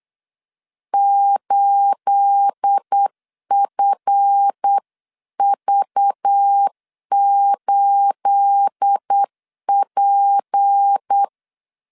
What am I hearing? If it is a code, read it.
Morse code, heard as a keyed tone, about 8 wpm, 800 Hz. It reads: 8FV8P